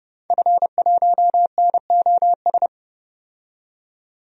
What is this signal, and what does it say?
Morse code, heard as a keyed tone, about 30 wpm, 700 Hz. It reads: F1NOS
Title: DXCC-100 Call Signs - 5 Characters - 30wpm